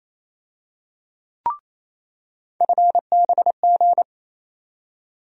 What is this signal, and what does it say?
Morse code, heard as a keyed tone, about 28 wpm, 700 Hz. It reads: FBG
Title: USA Call Sign Postfixes - 3 Letters 28wpm